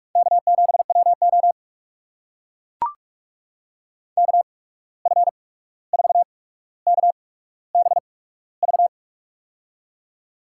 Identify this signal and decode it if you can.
Morse code, heard as a keyed tone, about 42 wpm, spaced as for 17 wpm, 700 Hz. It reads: K7WO KF4KBV
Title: USA Call Signs - 8x Character Spacing - 45wpm